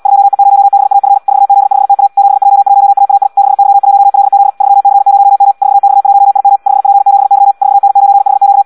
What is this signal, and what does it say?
Morse code, heard as a keyed tone, about 75 wpm, 800 Hz. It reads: 2E0BAF 4Z5TA 7J1AAI 6Y1LZ 3C0M 4X1IM 5P7Y 4N9HQ
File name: Audio practica CW / qrq-tx-78 2